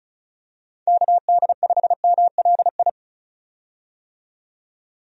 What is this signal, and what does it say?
Morse code, heard as a keyed tone, about 36 wpm, 700 Hz. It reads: KD5MLI
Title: USA Call Signs - 6 Characters - 35wpm